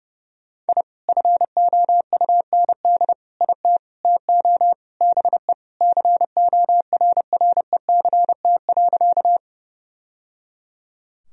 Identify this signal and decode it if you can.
Morse code, heard as a keyed tone, about 30 wpm, 700 Hz. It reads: I FOUND IT TO BE CORRECT.
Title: Sentences from Top 1000 Words - Review Part 2 - 30wpm